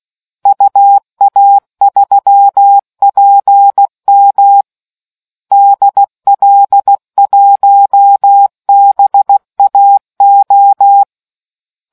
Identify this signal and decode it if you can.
Morse code, heard as a keyed tone, about 16 wpm, 800 Hz. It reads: UA3PM DL1BAO